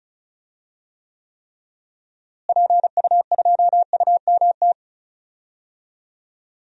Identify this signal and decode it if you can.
Morse code, heard as a keyed tone, about 34 wpm, 700 Hz. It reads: PU2UMT
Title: DXCC-100 Call Signs - 6 Characters - Repeated 3x - 35wpm